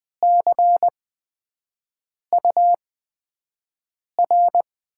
Morse code, heard as a keyed tone, about 20 wpm, spaced as for 5 wpm, 700 Hz. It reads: CUR
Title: USA Call Signs - 6 Characters - 8x Character Spacing - 20wpm